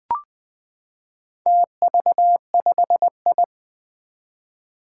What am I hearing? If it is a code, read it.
Morse code, heard as a keyed tone, about 21 wpm, 700 Hz. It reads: TV5I
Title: Four Challenging Letter-Number Combinations 20wpm